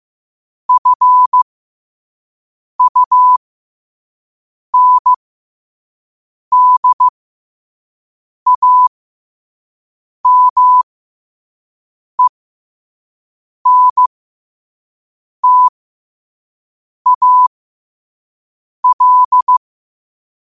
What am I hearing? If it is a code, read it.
Morse code, heard as a keyed tone, about 15 wpm, spaced as for 5 wpm, 1000 Hz. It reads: FUNDAMENTAL
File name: qotdfiles / 5wpm